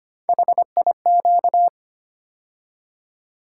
Morse code, heard as a keyed tone, about 25 wpm, 700 Hz. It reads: HIQ
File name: Three Letter-Number Combinations - 25wpm